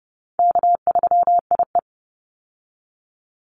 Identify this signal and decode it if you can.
Morse code, heard as a keyed tone, about 30 wpm, 700 Hz. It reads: K3IE
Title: Call Signs - 30wpm